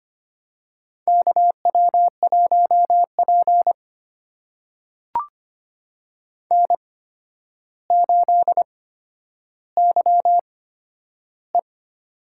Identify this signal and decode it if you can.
Morse code, heard as a keyed tone, about 25 wpm, spaced as for 11 wpm, 700 Hz. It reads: KW1P N8YE
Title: USA Call Signs - 4 Characters - 8x Character Spacing - 25wpm